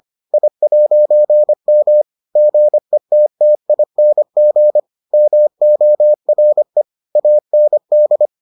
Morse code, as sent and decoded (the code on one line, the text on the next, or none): .. .----. -- / --. - - .. -. --. / -- --- .-. / .- -. -..
I'M GETTING MORE AND